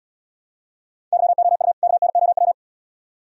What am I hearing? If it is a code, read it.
Morse code, heard as a keyed tone, about 77 wpm, 700 Hz. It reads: YOU KNOW